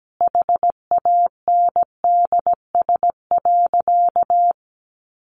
Morse code, heard as a keyed tone, about 17 wpm, 700 Hz. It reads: HANDS.
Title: A Sound Of Thunder 17wpm